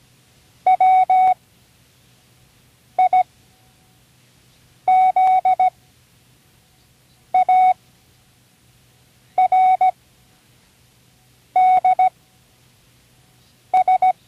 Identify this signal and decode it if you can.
Morse code, heard as a keyed tone, about 17 wpm, spaced as for 5 wpm, 740 Hz. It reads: WIZARDS
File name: CWResources / Learning Morse Code (from Tune in the World) / 29 words using all letters